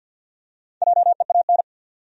Morse code, heard as a keyed tone, about 51 wpm, 700 Hz. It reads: JEAN
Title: USA Common Names - Max 4 Letters - 50wpm